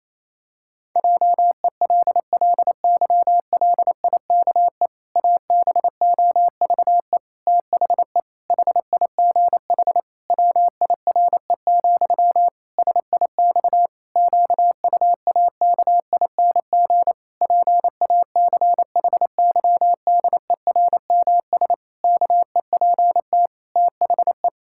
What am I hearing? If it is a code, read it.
Morse code, heard as a keyed tone, about 28 wpm, 700 Hz. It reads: JELLYLIKE ABOVE THE HIGH WIRE, SIX QUAKING PACHYDERMS KEPT THE
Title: Pangram Challenge v2 - 28wpm